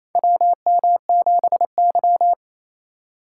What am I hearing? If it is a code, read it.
Morse code, heard as a keyed tone, about 28 wpm, 700 Hz. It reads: WM7Y
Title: USA Call Signs - 4 Characters - 3x Character Spacing - 28wpm